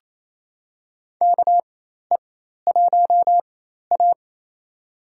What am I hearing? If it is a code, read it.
Morse code, heard as a keyed tone, about 27 wpm, spaced as for 13 wpm, 700 Hz. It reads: KE1A
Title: USA Call Signs - 4 Characters - 4x Character Spacing - 28wpm